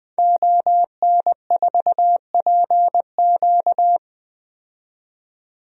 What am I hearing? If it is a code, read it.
Morse code, heard as a keyed tone, about 20 wpm, 700 Hz. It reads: ON4PQ
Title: DXCC-100 Call Signs - 5 Characters - 20wpm